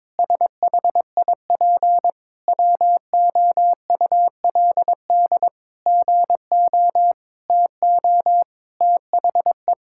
Morse code, heard as a keyed tone, about 22 wpm, 700 Hz. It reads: SHIP WOULD GO TO THE
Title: Sentences from Top 1000 Words - Review Part 6 - 22wpm